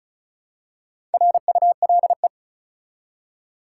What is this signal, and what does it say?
Morse code, heard as a keyed tone, about 34 wpm, 700 Hz. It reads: RULE
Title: Four Letter Words - 35wpm